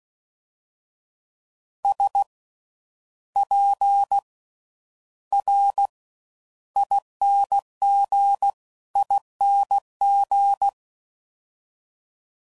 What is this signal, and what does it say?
Morse code, heard as a keyed tone, about 16 wpm, spaced as for 10 wpm, 800 Hz. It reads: SPRINGING
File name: Slow With Faster Ings